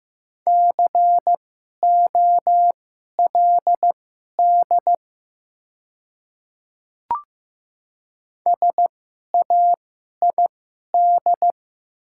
Morse code, heard as a keyed tone, about 15 wpm, spaced as for 11 wpm, 690 Hz. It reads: COLD SAID
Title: 2x Character Spacing - Top 500 Words - 15wpm